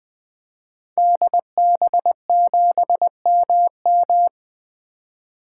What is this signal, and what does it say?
Morse code, heard as a keyed tone, about 21 wpm, 700 Hz. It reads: DB7MM
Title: 2023 SOTA Activator Call Signs - 20wpm